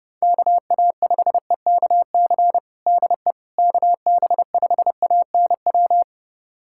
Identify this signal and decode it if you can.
Morse code, heard as a keyed tone, about 32 wpm, 700 Hz. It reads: KA5EKC DE KB5ANW
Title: USA Callsign DE Callsign - 6 Characters - 30wpm